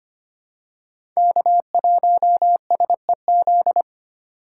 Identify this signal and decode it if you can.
Morse code, heard as a keyed tone, about 25 wpm, 700 Hz. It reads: K1SEZ
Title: USA Call Signs - 5 Characters - 2x Character Spacing - 25wpm